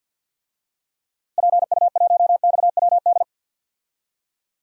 Morse code, heard as a keyed tone, about 53 wpm, 700 Hz. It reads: PU1XWD
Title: DXCC-100 Call Signs - 6 Characters - Repeated 2x - 50wpm